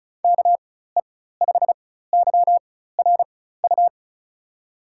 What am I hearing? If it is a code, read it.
Morse code, heard as a keyed tone, about 36 wpm, spaced as for 16 wpm, 700 Hz. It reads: KE5YRU
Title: USA Call Signs - 6 Characters - 4x Character Spacing - 35wpm